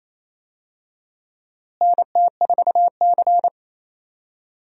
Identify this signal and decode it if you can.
Morse code, heard as a keyed tone, about 28 wpm, 700 Hz. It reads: NT4C
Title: USA Call Signs - 4 Characters - Repeated 2x - 28wpm